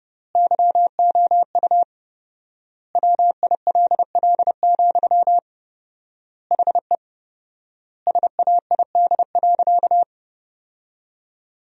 Morse code, heard as a keyed tone, about 30 wpm, 700 Hz. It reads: YOU WILL, HE SAID.
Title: 4x Word-Spacing - Sentences from Top 200 Words - 30wpm